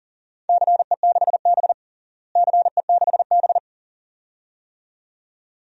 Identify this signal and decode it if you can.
Morse code, heard as a keyed tone, about 40 wpm, 700 Hz. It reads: CE6B CE6B
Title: DXCC-100 Call Signs - 4 Characters - Repeated 2x - 40wpm